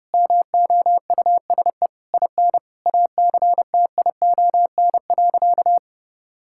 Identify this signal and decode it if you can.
Morse code, heard as a keyed tone, about 30 wpm, 700 Hz. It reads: MOUSE IN ACTION.
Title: Sentences from Top 1000 Words - Review Part 5 - 30wpm